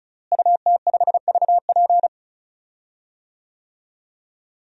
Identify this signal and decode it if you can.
Morse code, heard as a keyed tone, about 34 wpm, 700 Hz. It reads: UT5VP
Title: DXCC-100 Call Signs - 5 Characters - Repeated 3x - 35wpm